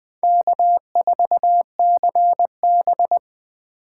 Morse code, heard as a keyed tone, about 20 wpm, 700 Hz. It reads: K4CB
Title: USA Call Signs - 4 Characters - 8x Character Spacing - 20wpm